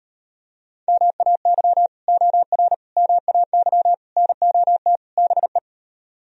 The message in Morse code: MAY OR MAY NOT BE

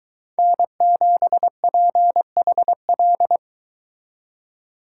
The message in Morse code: N7PHL

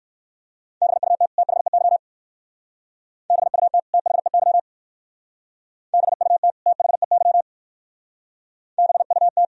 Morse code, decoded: BUT THEY BUT THEY BUT THEY BUT

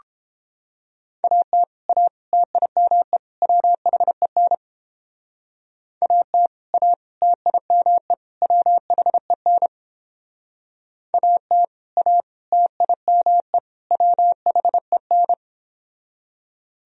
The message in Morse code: AT A TIME WHEN AT A TIME WHEN AT A TIME WHEN

